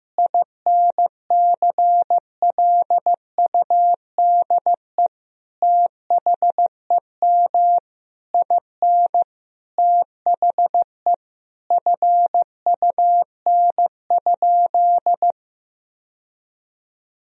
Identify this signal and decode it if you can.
Morse code, heard as a keyed tone, about 15 wpm, 700 Hz. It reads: INCLUDE THEM IN THE FUN?
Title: Sentences from Top 1000 Words - Review Part 3 - 15wpm